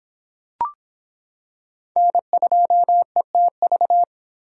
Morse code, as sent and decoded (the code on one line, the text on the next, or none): -. ..--- . - ...-
N2ETV